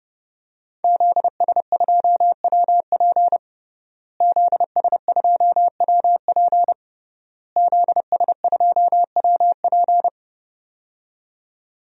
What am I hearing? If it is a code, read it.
Morse code, heard as a keyed tone, about 30 wpm, 700 Hz. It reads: ZS2WP ZS2WP ZS2WP
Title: DXCC-100 Call Signs - 5 Characters - Repeated 3x - 30wpm